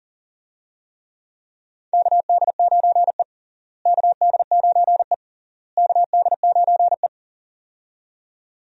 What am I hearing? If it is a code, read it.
Morse code, heard as a keyed tone, about 40 wpm, 700 Hz. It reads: KD9E KD9E KD9E
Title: USA Call Signs - 4 Characters - Repeated 3x - 40wpm